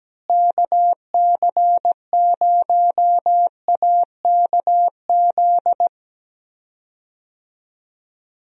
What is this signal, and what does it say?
Morse code, heard as a keyed tone, about 17 wpm, 700 Hz. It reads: KC0AKZ
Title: USA Call Signs - 6 Characters - Repeated 2x - 17wpm